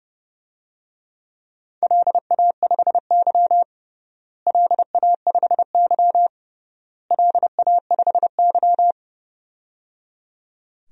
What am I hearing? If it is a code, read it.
Morse code, heard as a keyed tone, about 30 wpm, 700 Hz. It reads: LA5Y LA5Y LA5Y
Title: DXCC-100 Call Signs - 4 Characters - Repeated 3x - 30wpm